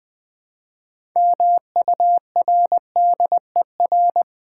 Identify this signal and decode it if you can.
Morse code, heard as a keyed tone, about 21 wpm, 690 Hz. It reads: MURDER
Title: Build A Word - 2 Letters At A Time - 6-Letter Words - 20wpm